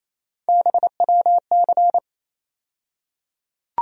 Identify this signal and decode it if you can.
Morse code, heard as a keyed tone, about 28 wpm, 700 Hz. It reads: BWC
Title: USA Call Sign Postfixes - 3 Letters 28wpm